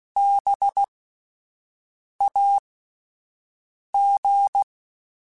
Morse code, heard as a keyed tone, about 16 wpm, spaced as for 5 wpm, 800 Hz. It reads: BAG
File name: Spaced Alphas / Spaced Alphas mp3s / Bs sp @10